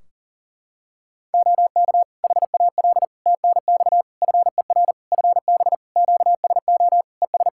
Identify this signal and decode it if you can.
Morse code, heard as a keyed tone, about 40 wpm, 700 Hz. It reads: OK HAL TNX FER FB QSO ES
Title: Practice QSO - Exchange 4 - Ending - 40wpm